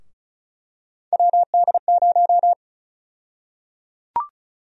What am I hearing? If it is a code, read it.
Morse code, heard as a keyed tone, about 36 wpm, 700 Hz. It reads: WD0
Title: USA Call Sign Prefixes - 35wpm